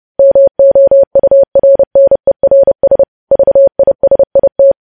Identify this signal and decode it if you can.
Morse code, heard as a keyed tone, about 30 wpm, 550 Hz. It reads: MOURNERS VISIT
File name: podcast / News headlines-1x-30x30